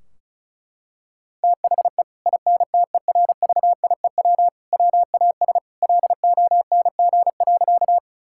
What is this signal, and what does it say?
Morse code, heard as a keyed tone, about 34 wpm, 700 Hz. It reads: THE INTERVIEW WAS LONG.
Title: Sentences from Top 1600 Words 35wpm